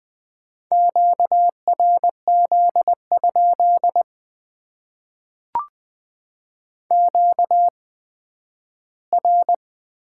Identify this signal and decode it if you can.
Morse code, heard as a keyed tone, about 20 wpm, spaced as for 12 wpm, 700 Hz. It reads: QRZ? QR
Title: Top 90 QSO Elements - 8x Character Spacing - 20wpm